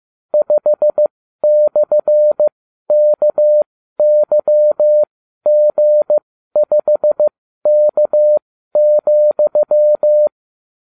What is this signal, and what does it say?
Morse code, heard as a keyed tone, about 15 wpm, spaced as for 12 wpm, 600 Hz. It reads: 5/KYG5K,